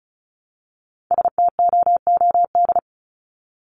Morse code, heard as a keyed tone, about 34 wpm, 700 Hz. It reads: STOOD
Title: Top 500 Words Repeated 35wpm